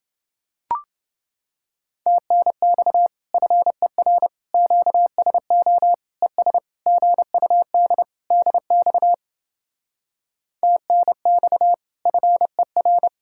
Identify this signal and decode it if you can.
Morse code, heard as a keyed tone, about 28 wpm, 700 Hz. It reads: TNX FER QSO ES GUD DX TNX FER